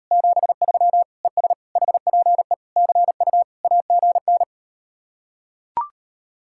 Morse code, heard as a keyed tone, about 39 wpm, 700 Hz. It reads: 73 ES HPE CU AGN